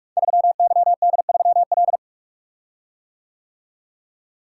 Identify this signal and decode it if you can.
Morse code, heard as a keyed tone, about 44 wpm, 700 Hz. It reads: 3YD3L
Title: Code Groups - 5 Letter:Numbers - 45wpm